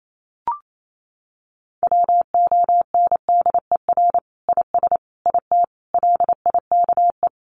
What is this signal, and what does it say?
Morse code, heard as a keyed tone, about 28 wpm, 700 Hz. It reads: WONDER IS IT LIKE
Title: Sentences from Top 500 Words 28wpm